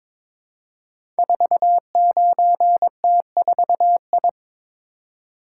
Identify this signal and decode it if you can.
Morse code, heard as a keyed tone, about 22 wpm, 700 Hz. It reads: 49T4I
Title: Code Groups - 5 Letter:Numbers - 22wpm